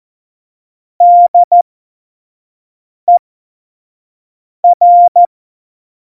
Morse code, heard as a keyed tone, about 14 wpm, spaced as for 5 wpm, 700 Hz. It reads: DER
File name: Morsekurs / Witze / Witz 24. November 2020, 25 F70